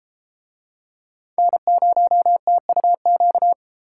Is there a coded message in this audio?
Yes: Morse code from 1.4 to 3.5 s, heard as a keyed tone, about 32 wpm, 700 Hz. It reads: N0TUQ